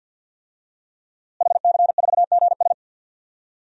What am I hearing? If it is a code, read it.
Morse code, heard as a keyed tone, about 50 wpm, 700 Hz. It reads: HC4GS
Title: DXCC-100 Call Signs - 5 Characters - Repeated 3x - 50wpm